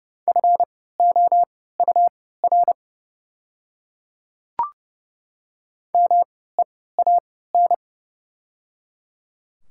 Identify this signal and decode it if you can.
Morse code, heard as a keyed tone, about 30 wpm, spaced as for 17 wpm, 690 Hz. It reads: FOUR MEAN
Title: 3x Character Spacing - Top 300 Words - 30wpm